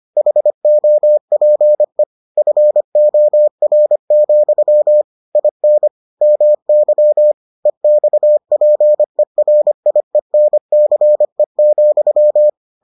Morse code, heard as a keyed tone, about 25 wpm, 600 Hz. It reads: HOPE FOR, IN MY EXPERIENCE,